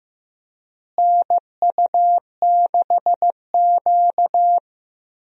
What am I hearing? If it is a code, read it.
Morse code, heard as a keyed tone, about 15 wpm, 700 Hz. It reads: NU6Q
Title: USA Call Signs - 4 Characters - Repeated 3x - 15wpm